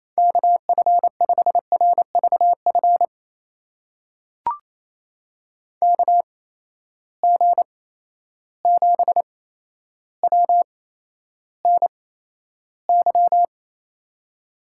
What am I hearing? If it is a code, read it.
Morse code, heard as a keyed tone, about 28 wpm, spaced as for 12 wpm, 700 Hz. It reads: KF5RVF KG7WNY
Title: USA Call Signs - 6 Characters - 8x Character Spacing - 28wpm